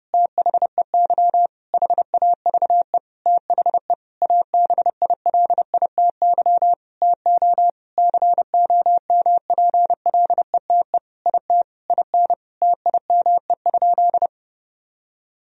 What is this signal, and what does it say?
Morse code, heard as a keyed tone, about 30 wpm, 700 Hz. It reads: THEY HAVE THE ABILITY TO COMPLETE IT IN TIME?
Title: Sentences from Top 1000 Words - Review Part 3 - 30wpm